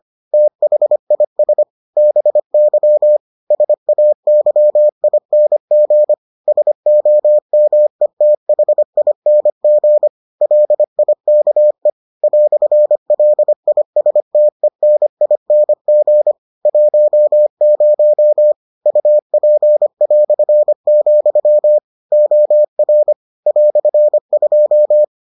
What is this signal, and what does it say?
Morse code, heard as a keyed tone, about 25 wpm, 600 Hz. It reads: THIS BY SAYING SOMETHING LIKE "LISTENING 10 UP", OR "2